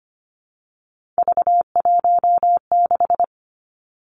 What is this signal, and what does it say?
Morse code, heard as a keyed tone, about 25 wpm, 700 Hz. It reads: V16
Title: Three Challenging Letter-Number Combinations 25wpm